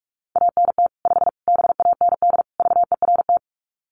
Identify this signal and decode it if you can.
Morse code, heard as a keyed tone, about 44 wpm, 700 Hz. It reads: ANT 5 BAND VERT
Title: Common QSO Phrases 45wpm